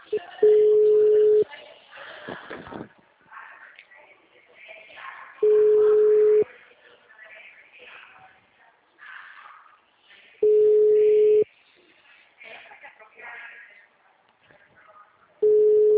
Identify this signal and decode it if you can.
Ringback tone (Europe, 425 Hz).